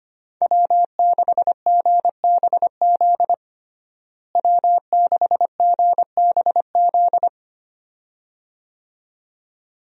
Morse code, heard as a keyed tone, about 25 wpm, 700 Hz. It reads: W6GBZ W6GBZ
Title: USA Call Signs - 5 Characters - Repeated 2x - 25wpm